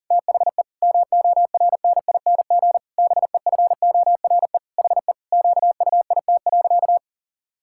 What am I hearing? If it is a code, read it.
Morse code, heard as a keyed tone, about 40 wpm, 700 Hz. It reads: THE MORNING BEFORE HE QUIT.
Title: Pangram Challenge v2 - 40wpm